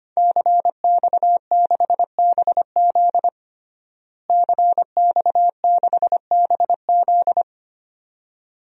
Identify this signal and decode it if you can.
Morse code, heard as a keyed tone, about 25 wpm, 700 Hz. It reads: CX6BZ CX6BZ
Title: DXCC-100 Call Signs - 5 Characters - Repeated 2x - 25wpm